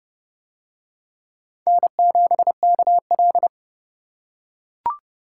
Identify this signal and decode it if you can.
Morse code, heard as a keyed tone, about 28 wpm, 700 Hz. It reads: N7KL